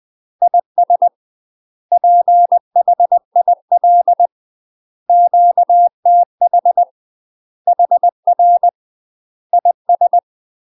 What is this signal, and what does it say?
Morse code, heard as a keyed tone, about 20 wpm, 700 Hz. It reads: IS PHIL QTH HR IS
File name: UK MASTER QSOs TRAINING / 20WPM / UK MASTER QSOs 0052